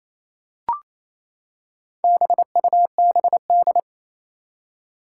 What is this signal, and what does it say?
Morse code, heard as a keyed tone, about 28 wpm, 700 Hz. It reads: BUBD
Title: Four Challenging Letter-Number Combinations 28wpm